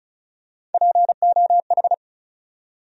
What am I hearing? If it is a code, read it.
Morse code, heard as a keyed tone, about 34 wpm, 700 Hz. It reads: POH
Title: USA Call Sign Postfixes - 3 Letters 35wpm